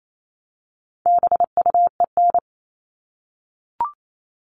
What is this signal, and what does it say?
Morse code, heard as a keyed tone, about 28 wpm, 700 Hz. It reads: BUEN
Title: All Cut Numbers - 4-Digit 28wpm